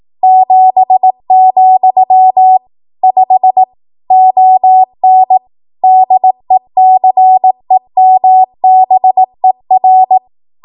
Morse code, heard as a keyed tone, about 18 wpm, 750 Hz. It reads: 7, 5 ON DECEMBER
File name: Archive / 18 WPM / 18WPM